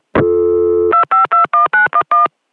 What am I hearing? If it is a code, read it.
DTMF, heard as telephone touch tones, tones of about 130 ms. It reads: 2521911